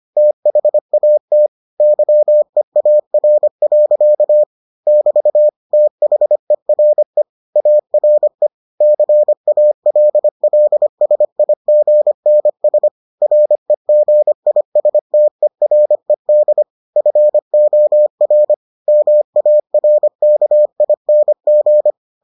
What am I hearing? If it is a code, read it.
Morse code, heard as a keyed tone, about 25 wpm, 600 Hz. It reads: THAT YEAR. <BT> THERE ARE CALLSIGNS REGISTERED FOR MARKING